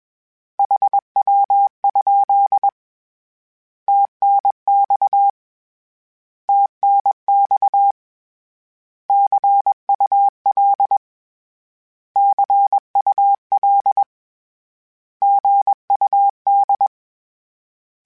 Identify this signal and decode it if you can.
Morse code, heard as a keyed tone, about 21 wpm, 800 Hz. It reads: HW? TNX TNX CUL CUL GUD
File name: Abreviations / Abreviations mp3s / Abreviations@20